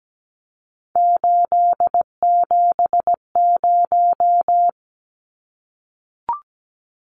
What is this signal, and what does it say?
Morse code, heard as a keyed tone, about 17 wpm, 700 Hz. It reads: 870
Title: Numbers - 3-Digit 17wpm